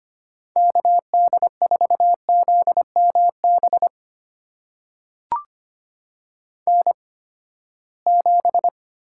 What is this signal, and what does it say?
Morse code, heard as a keyed tone, about 25 wpm, spaced as for 17 wpm, 700 Hz. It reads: KD4ZMB N7
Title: USA Call Signs - 8x Character Spacing - 25wpm